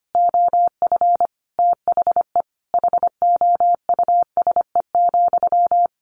Morse code, heard as a keyed tone, about 25 wpm, 700 Hz. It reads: OF THE HOUSE,
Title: Sentences from Top 1200 Words 25wpm